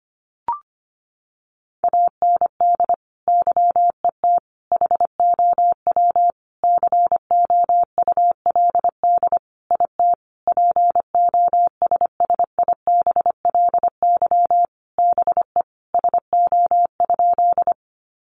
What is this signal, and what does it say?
Morse code, heard as a keyed tone, about 25 wpm, 700 Hz. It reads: AND YET HOW COULD IT POSSIBLY BE SO?
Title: Sentences from Top 1300 Words 25wpm